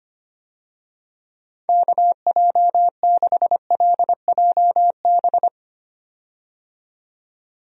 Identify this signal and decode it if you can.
Morse code, heard as a keyed tone, about 25 wpm, 700 Hz. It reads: KJ6LJB
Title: USA Call Signs - 6 Characters - 25wpm